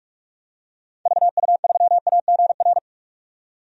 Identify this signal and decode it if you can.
Morse code, heard as a keyed tone, about 42 wpm, 700 Hz. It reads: VU3AGR
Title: DXCC-100 Call Signs - 6 Characters - 45wpm